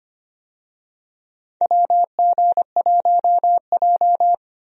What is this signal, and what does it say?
Morse code, heard as a keyed tone, about 25 wpm, 700 Hz. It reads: WG1J
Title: USA Call Signs - 4 Characters - Repeated 3x - 25wpm